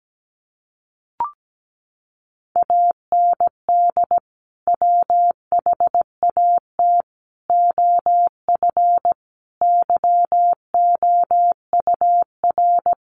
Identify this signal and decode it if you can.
Morse code, heard as a keyed tone, about 17 wpm, 700 Hz. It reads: AND WHAT OF YOUR
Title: Sentences from Top 900 Words 17wpm